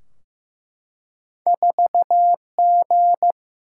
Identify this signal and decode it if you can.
Morse code, heard as a keyed tone, about 15 wpm, 700 Hz. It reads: 4G
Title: Single-Letter Single-Number 15wpm